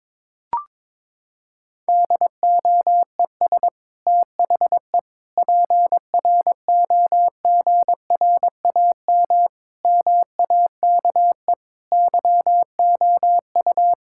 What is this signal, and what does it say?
Morse code, heard as a keyed tone, about 22 wpm, 700 Hz. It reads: DOES THE PROGRAM MAKE YOU
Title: Sentences from Top 1000 Words - Review Part 5 - 22wpm